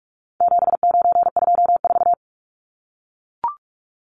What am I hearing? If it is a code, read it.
Morse code, heard as a keyed tone, about 44 wpm, 700 Hz. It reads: 7924
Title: Numbers - 4-Digit 45wpm